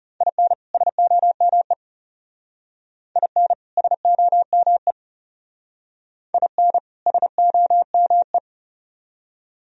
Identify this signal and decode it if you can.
Morse code, heard as a keyed tone, about 34 wpm, 700 Hz. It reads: IN SOME IN SOME IN SOME
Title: Speed-Racing - Sets of 2 Words 30wpm